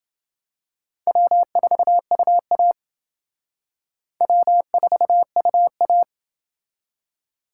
Morse code, heard as a keyed tone, about 28 wpm, 700 Hz. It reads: W4UA W4UA